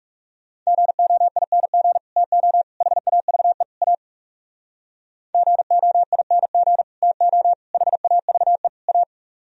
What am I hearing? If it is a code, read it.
Morse code, heard as a keyed tone, about 38 wpm, 700 Hz. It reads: GOING TO HAVE A GOING TO HAVE A